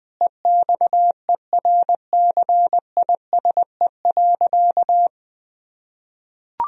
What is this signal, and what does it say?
Morse code, heard as a keyed tone, about 21 wpm, 700 Hz. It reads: EXERCISE.
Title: Sentences from Top 1000 Words - Review Part 4 - 20wpm